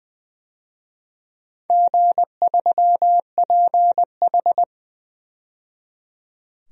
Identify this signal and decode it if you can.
Morse code, heard as a keyed tone, about 20 wpm, 700 Hz. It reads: G3PH
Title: DXCC-100 Call Signs - 4 Characters - 20wpm